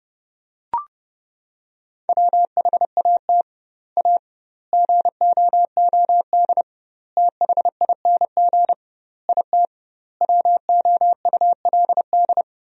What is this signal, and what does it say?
Morse code, heard as a keyed tone, about 32 wpm, 700 Hz. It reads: WHAT A GOOD THING IT WOULD
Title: 2x Word-Spacing - Sentences from Top 200 Words - 30wpm